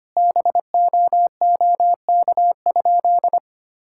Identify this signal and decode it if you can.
Morse code, heard as a keyed tone, about 25 wpm, 700 Hz. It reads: BOOK?
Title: Sentences from Top 1000 Words - Review Part 5 - 25wpm